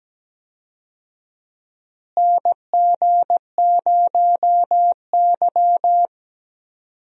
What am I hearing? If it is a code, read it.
Morse code, heard as a keyed tone, about 17 wpm, 700 Hz. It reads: NG0Y
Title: USA Call Signs - 4 Characters - Repeated 2x - 17wpm